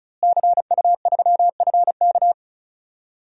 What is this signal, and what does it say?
Morse code, heard as a keyed tone, about 36 wpm, 700 Hz. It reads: CU3FK
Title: DXCC-100 Call Signs - 5 Characters - Repeated 3x - 35wpm